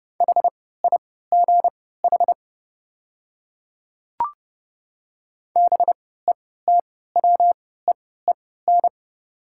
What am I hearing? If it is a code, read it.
Morse code, heard as a keyed tone, about 31 wpm, spaced as for 17 wpm, 700 Hz. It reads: HIGH BETWEEN
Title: 3x Character Spacing - Top 300 Words - 30wpm